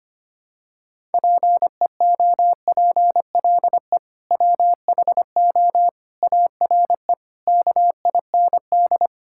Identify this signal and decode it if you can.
Morse code, heard as a keyed tone, about 25 wpm, 700 Hz. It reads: PEOPLE WHO ARE KIND
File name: practice / 25wpm / Sentences with 3-Word Phrases - 25wpm